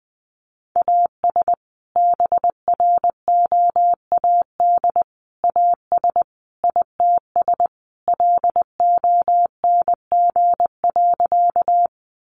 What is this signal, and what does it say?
Morse code, heard as a keyed tone, about 20 wpm, 700 Hz. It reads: AS BROAD AS ITS LONG.
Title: Sentences from Top 1000 Words 20wpm